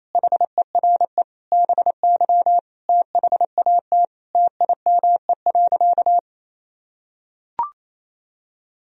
Morse code, heard as a keyed tone, about 28 wpm, 700 Hz. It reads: HERE BY THAT TIME.
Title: Sentences from Top 1000 Words - Review Part 2 - 28wpm